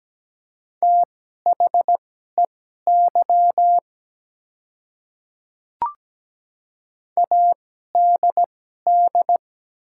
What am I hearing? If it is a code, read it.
Morse code, heard as a keyed tone, about 17 wpm, spaced as for 12 wpm, 700 Hz. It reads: THEY ADD
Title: 2x Character Spacing - Top 300 Words - 17wpm